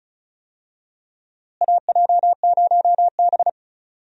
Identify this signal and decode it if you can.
Morse code, heard as a keyed tone, about 34 wpm, 700 Hz. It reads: AJ0B
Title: USA Call Signs - 4 Characters - Repeated 3x - 35wpm